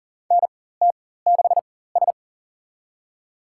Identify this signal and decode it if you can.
Morse code, heard as a keyed tone, about 40 wpm, spaced as for 19 wpm, 700 Hz. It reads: NT6S
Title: USA Call Signs - 4 Characters - 4x Character Spacing - 40wpm